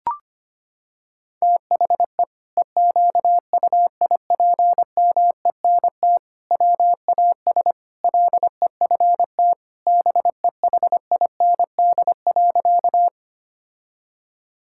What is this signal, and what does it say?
Morse code, heard as a keyed tone, about 25 wpm, 700 Hz. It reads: THE EQUIPMENT WAS LEFT BEHIND.
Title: Sentences from Top 1000 Words - Review Part 5 - 25wpm